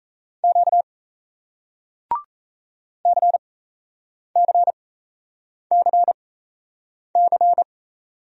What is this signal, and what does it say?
Morse code, heard as a keyed tone, about 37 wpm, spaced as for 8 wpm, 700 Hz. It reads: Q CCCC